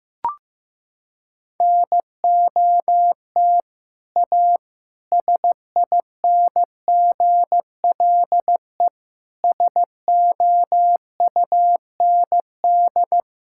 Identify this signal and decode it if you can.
Morse code, heard as a keyed tone, about 15 wpm, 700 Hz. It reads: NOT A SINGLE SOUND
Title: Sentences from Top 1000 Words - Review Part 5 - 15wpm